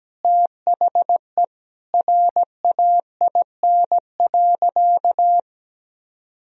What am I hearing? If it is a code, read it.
Morse code, heard as a keyed tone, about 17 wpm, 700 Hz. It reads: THE RAIN.
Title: Sentences from Top 1000 Words - Review Part 1 - 17wpm